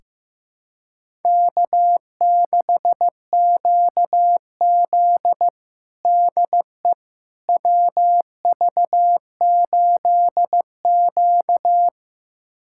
Morse code, heard as a keyed tone, about 15 wpm, 700 Hz. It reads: K6QZ DE WV8Q
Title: USA Callsign DE Callsign - 4 Characters - 15wpm